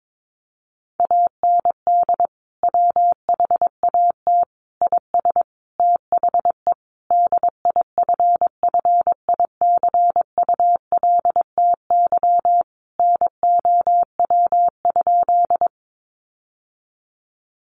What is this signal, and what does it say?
Morse code, heard as a keyed tone, about 22 wpm, 700 Hz. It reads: AND WHAT IS THE DIFFICULTY NOW?
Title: Sentences from Top 1500 Words 22wpm